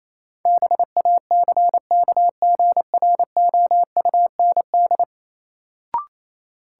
Morse code, heard as a keyed tone, about 28 wpm, 700 Hz. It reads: BACKGROUND
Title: Sentences from Top 1300 Words - New Word Drill - 28wpm